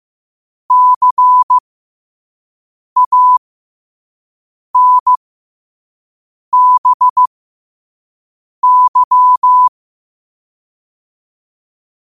Morse code, heard as a keyed tone, about 15 wpm, spaced as for 5 wpm, 1000 Hz. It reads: CANBY
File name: qotdfiles / 5wpm